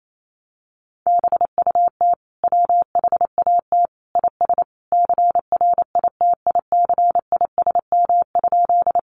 Morse code, heard as a keyed tone, about 28 wpm, 700 Hz. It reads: BUT WHAT IS CRITICISM?
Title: Sentences from Top 1300 Words 28wpm